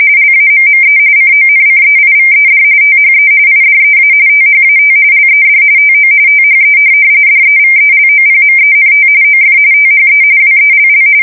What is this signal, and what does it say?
RTTY, about 45.45 baud. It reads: LAMME CONCEIVED THE IDEA THAT LED TO THE DEVELOPMENT OF THE WELL